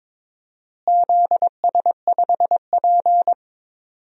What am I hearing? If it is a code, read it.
Morse code, heard as a keyed tone, about 22 wpm, 700 Hz. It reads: ZS5P
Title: DXCC-100 Call Signs - 4 Characters - Repeated 2x - 22wpm